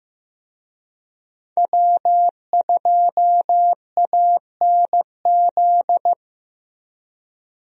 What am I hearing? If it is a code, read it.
Morse code, heard as a keyed tone, about 15 wpm, 700 Hz. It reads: W2ANZ
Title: USA Call Signs - 5 Characters - Repeated 3x - 15wpm